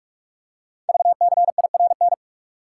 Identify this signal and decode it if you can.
Morse code, heard as a keyed tone, about 44 wpm, 700 Hz. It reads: VCIRN
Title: Code Groups - 5 Letters - 45wpm